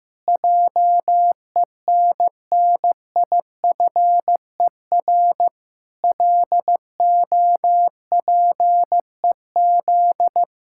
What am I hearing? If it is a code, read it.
Morse code, heard as a keyed tone, about 15 wpm, 700 Hz. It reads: JENNIFER LOPEZ
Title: Pangram Challenge v2 - 15wpm